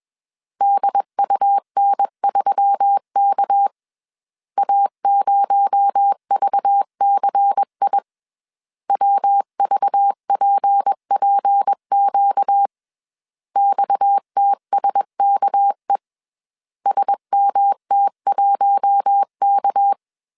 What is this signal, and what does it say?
Morse code, heard as a keyed tone, about 21 wpm, 800 Hz. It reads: DUN3K A0VCI WVPPQ XTSKE SMT1K
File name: CW / DET / 1 6 1500C S B 20wpm 2048